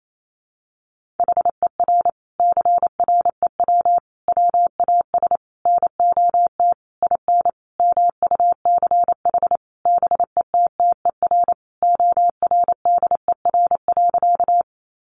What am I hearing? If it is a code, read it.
Morse code, heard as a keyed tone, about 28 wpm, 700 Hz. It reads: HER CREW WAS NOT IN MUCH BETTER ORDER.
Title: Sentences from Top 1500 Words 28wpm